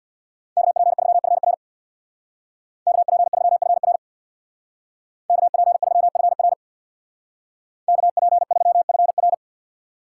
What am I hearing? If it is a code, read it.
Morse code, heard as a keyed tone, about 61 wpm, 700 Hz. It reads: KP3FR KP3FR KP3FR KP3FR
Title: Speed Racing - USA 2x2 Call Signs 50wpm